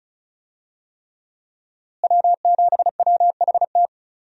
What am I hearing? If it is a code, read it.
Morse code, heard as a keyed tone, about 36 wpm, 700 Hz. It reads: W7WHT
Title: USA Call Signs - 5 Characters - Repeated 3x - 35wpm